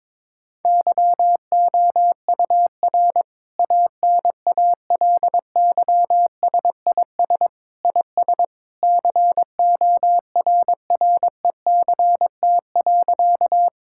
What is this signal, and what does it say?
Morse code, heard as a keyed tone, about 22 wpm, 700 Hz. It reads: YOUR ANALYSIS IS CORRECT.
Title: Sentences from Top 1000 Words - Review Part 5 - 22wpm